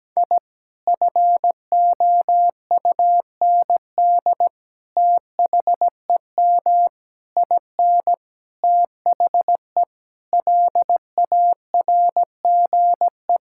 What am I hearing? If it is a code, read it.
Morse code, heard as a keyed tone, about 17 wpm, 700 Hz. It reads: I FOUND THEM IN THE LARGE
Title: Sentences from Top 1000 Words - Review Part 5 - 17wpm